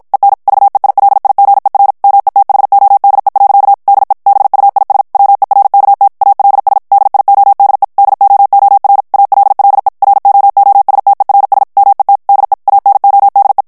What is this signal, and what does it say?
Morse code, heard as a keyed tone, about 56 wpm, 800 Hz. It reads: EN VEILIGER METHODE. DE BUIS WERKT ALS DIODE DOOR ALLE ROOSTERS MET DE ANODE
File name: HOLLANDAIS texte clair 2 vitesse 50 mots minute